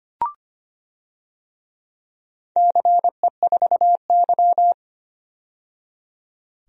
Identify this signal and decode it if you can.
Morse code, heard as a keyed tone, about 25 wpm, 700 Hz. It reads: CE4Y
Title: DXCC-100 Call Signs - 4 Characters - 25wpm